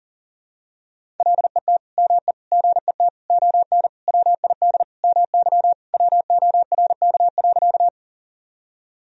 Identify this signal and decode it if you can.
Morse code, heard as a keyed tone, about 40 wpm, 700 Hz. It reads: LET ME GET ON WID MY WORK.
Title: Sentences from Top 300 Words and Abbreviations 40wpm